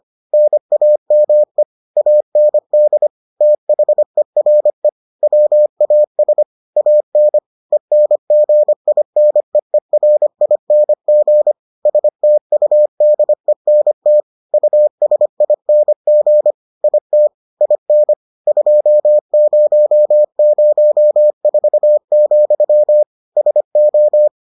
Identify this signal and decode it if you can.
Morse code, heard as a keyed tone, about 25 wpm, 600 Hz. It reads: NAME AND THERE WAS AN ENGINEERING STUDENT USING IT IN 2004, SO